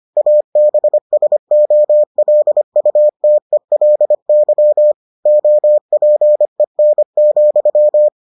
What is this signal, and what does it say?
Morse code, heard as a keyed tone, about 25 wpm, 600 Hz. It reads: ABSOLUTELY OPEN,